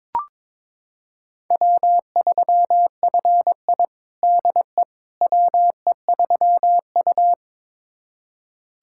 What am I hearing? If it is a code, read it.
Morse code, heard as a keyed tone, about 22 wpm, 700 Hz. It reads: W3FI DE WE3U
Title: USA Callsign DE Callsign - 4 Characters - 22wpm